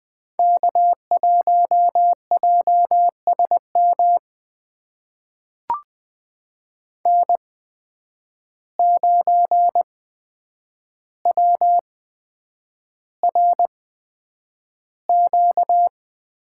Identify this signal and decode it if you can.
Morse code, heard as a keyed tone, about 20 wpm, spaced as for 9 wpm, 700 Hz. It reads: K1JSM N9WRQ
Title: USA Call Signs - 5 Characters - 8x Character Spacing - 20wpm